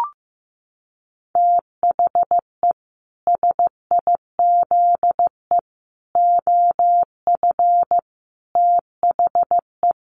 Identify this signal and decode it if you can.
Morse code, heard as a keyed tone, about 15 wpm, 700 Hz. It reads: THE SIZE OF THE
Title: Sets of 4 Words 15wpm